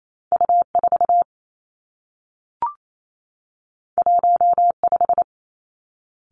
Morse code, heard as a keyed tone, about 28 wpm, 700 Hz. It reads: U 4 1 5
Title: Two Challenging Letter-Number Combinations 28wpm